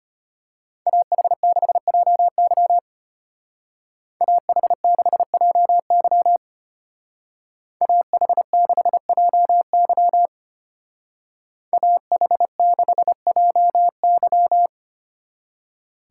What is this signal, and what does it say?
Morse code, heard as a keyed tone, about 33 wpm, 700 Hz. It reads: AH6JY AH6JY AH6JY AH6JY